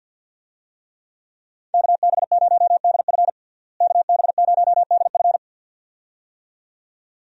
Morse code, heard as a keyed tone, about 47 wpm, 700 Hz. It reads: KB0DF KB0DF
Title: USA Call Signs - 5 Characters - Repeated 2x - 50wpm